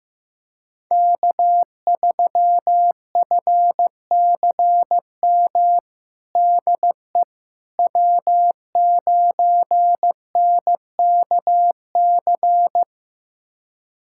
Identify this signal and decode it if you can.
Morse code, heard as a keyed tone, about 15 wpm, 700 Hz. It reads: K3FCM DE W9NKC
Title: USA Callsign DE Callsign - 5 Characters - 15wpm